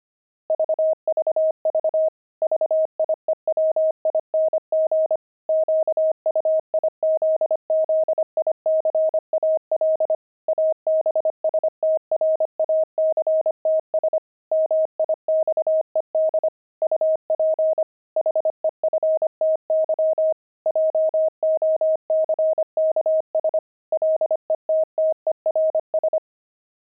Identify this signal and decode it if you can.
Morse code, heard as a keyed tone, about 25 wpm, 630 Hz. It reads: VVV VIEWING QUIZZICAL ABSTRACTS MIXED UP HEFTY JOCKS LETTERS
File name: pangram-fs-25-cs-25-jocks